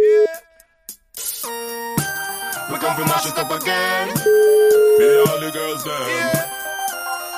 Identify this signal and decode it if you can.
Ringback tone (Europe, 425 Hz), with noise.